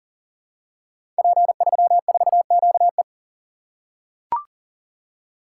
Morse code, heard as a keyed tone, about 39 wpm, 700 Hz. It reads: P34QE
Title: Code Groups - 5 Letter:Numbers - 40wpm